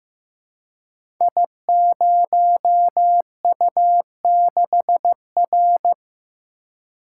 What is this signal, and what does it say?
Morse code, heard as a keyed tone, about 15 wpm, 700 Hz. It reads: I0U6R
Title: Code Groups - 5 Letter:Numbers - 15wpm